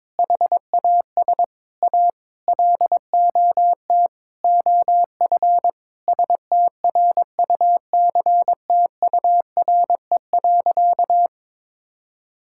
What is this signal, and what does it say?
Morse code, heard as a keyed tone, about 22 wpm, 700 Hz. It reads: HAS A LOT OF STRUCTURE.
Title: Sentences from Top 1000 Words - Review Part 6 - 22wpm